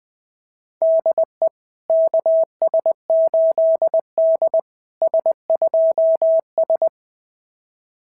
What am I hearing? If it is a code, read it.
Morse code, heard as a keyed tone, about 20 wpm, 660 Hz. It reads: DE KS8D S2S
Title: SOTA QSO Practice 20wpm